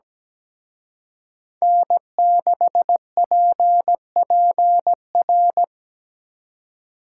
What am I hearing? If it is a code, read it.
Morse code, heard as a keyed tone, about 20 wpm, 700 Hz. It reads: N6PPR
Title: Speed Racing - USA 1x3 Call Signs 17wpm